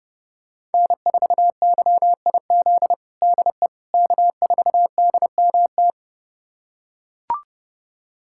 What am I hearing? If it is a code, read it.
Morse code, heard as a keyed tone, about 30 wpm, 700 Hz. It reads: N4YIZ DE K4DMT
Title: USA Callsign DE Callsign - 5 Characters - 30wpm